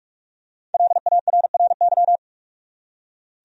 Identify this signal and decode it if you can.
Morse code, heard as a keyed tone, about 46 wpm, 700 Hz. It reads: LARRY
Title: USA Common Names - Max 5 Letters - 45wpm